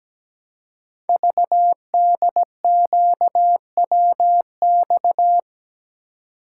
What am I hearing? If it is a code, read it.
Morse code, heard as a keyed tone, about 17 wpm, 700 Hz. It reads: VDQWX
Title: Code Groups - 5 Letter:Numbers - 17wpm